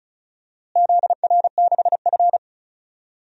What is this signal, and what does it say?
Morse code, heard as a keyed tone, about 36 wpm, 700 Hz. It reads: ZR6F
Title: DXCC-100 Call Signs - 4 Characters - Repeated 2x - 35wpm